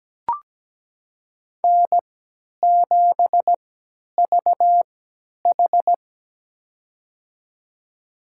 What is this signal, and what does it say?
Morse code, heard as a keyed tone, about 17 wpm, spaced as for 10 wpm, 700 Hz. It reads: N7VH
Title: USA Call Signs - 3x Character Spacing - 17wpm